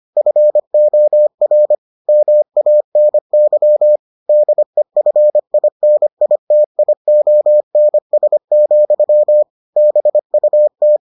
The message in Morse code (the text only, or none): FOR MANY DEFINITIONS, BUT